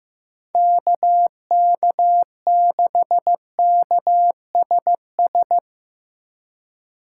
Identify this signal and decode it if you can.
Morse code, heard as a keyed tone, about 15 wpm, 700 Hz. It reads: KK6KSS
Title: USA Call Signs - 6 Characters - 3x Character Spacing - 15wpm